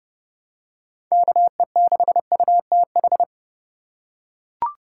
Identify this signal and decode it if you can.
Morse code, heard as a keyed tone, about 30 wpm, 700 Hz. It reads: KE6UTH
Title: USA Call Signs - 6 Characters - 4x Character Spacing - 30wpm